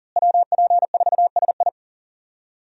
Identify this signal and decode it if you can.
Morse code, heard as a keyed tone, about 38 wpm, 700 Hz. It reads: WP4SI
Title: USA Call Signs - 5 Characters - Repeated 3x - 40wpm